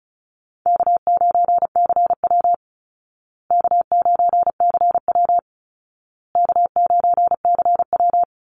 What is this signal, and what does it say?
Morse code, heard as a keyed tone, about 34 wpm, 700 Hz. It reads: K9CW K9CW K9CW
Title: Call Signs Repeated 2x - 35wpm